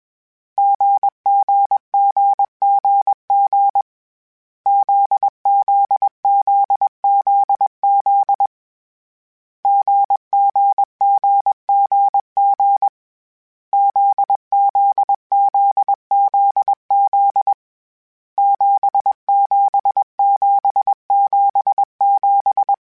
Morse code, heard as a keyed tone, about 21 wpm, 800 Hz. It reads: GGGGG ZZZZZ GGGGG ZZZZZ 77777
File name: Short Runs / Short Runs-mp3s / GZ GZ7 @20